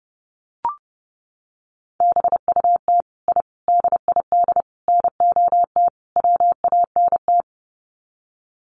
Morse code, heard as a keyed tone, about 30 wpm, 700 Hz. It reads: BUT I DID NOT WANT
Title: Sets of 5 Words Encore 30wpm